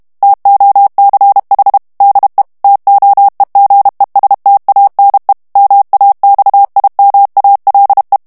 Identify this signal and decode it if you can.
Morse code, heard as a keyed tone, about 32 wpm, 800 Hz. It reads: TOCH DE TOEGESTANE MAXIMALE
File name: HOLLANDAIS texte clair 2 vitesse 30 mots minute